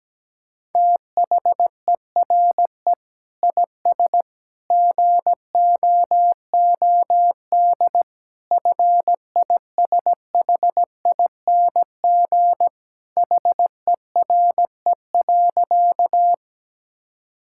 Morse code, heard as a keyed tone, about 17 wpm, 700 Hz. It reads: THERE IS GOOD FISHING HERE.
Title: Sentences from Top 1000 Words - Review Part 5 - 17wpm